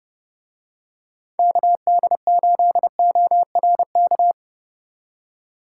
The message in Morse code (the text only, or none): KD8ORK